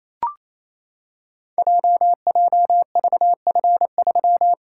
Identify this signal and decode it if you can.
Morse code, heard as a keyed tone, about 27 wpm, 700 Hz. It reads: JJVF3
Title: Code Groups - 5 Letter:Numbers - 28wpm